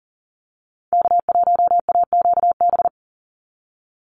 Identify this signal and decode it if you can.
Morse code, heard as a keyed tone, about 40 wpm, 700 Hz. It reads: K1AQB
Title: Call Signs Repeated 2x - 40wpm